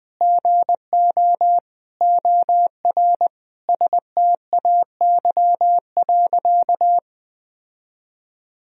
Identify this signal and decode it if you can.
Morse code, heard as a keyed tone, about 20 wpm, 700 Hz. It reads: GO OR STAY.
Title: Sentences from Top 1000 Words - Review Part 3 - 20wpm